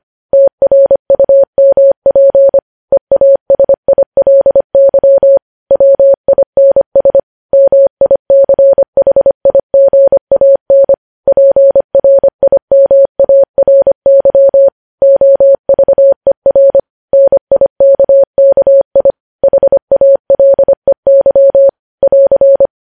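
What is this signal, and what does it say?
Morse code, heard as a keyed tone, about 25 wpm, 550 Hz. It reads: TRUMP EASILY WINS MICHIGAN PRIMARY OVER NIKKI HALEY <AR>
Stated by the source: Morse code news headlines sent at 25 wpm.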